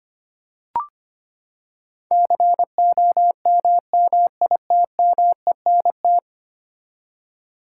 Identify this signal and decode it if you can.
Morse code, heard as a keyed tone, about 25 wpm, 700 Hz. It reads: COMMITMENT
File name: Ten Letter Words - 25wpm